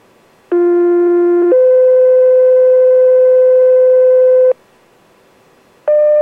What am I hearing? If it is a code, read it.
Two-tone page, heard as a one-second tone, then a three-second tone, about 350 then 510 Hz. It is heard from 0.5 to 4.5 s.